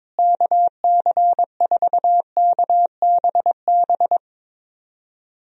Morse code, heard as a keyed tone, about 22 wpm, 700 Hz. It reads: KC4KBB
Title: USA Call Signs - 6 Characters - Repeated 3x - 22wpm